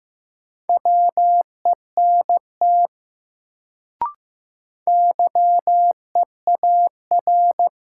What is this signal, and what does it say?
Morse code, heard as a keyed tone, about 15 wpm, 700 Hz. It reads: WENT YEAR
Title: Sentences from Top 200 Words - New Word Drill - 15wpm